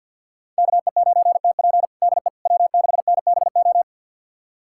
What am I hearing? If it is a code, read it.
Morse code, heard as a keyed tone, about 48 wpm, 700 Hz. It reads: KE9TP DE W6NBO
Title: USA Callsign DE Callsign - 5 Characters - 50wpm